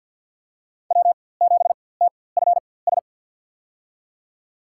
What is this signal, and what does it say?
Morse code, heard as a keyed tone, about 47 wpm, spaced as for 23 wpm, 700 Hz. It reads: W7TFS
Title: USA Call Signs - 5 Characters - 4x Character Spacing - 50wpm